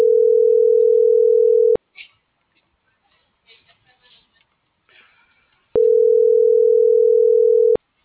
Ringback tone (North America), 440 + 480 Hz.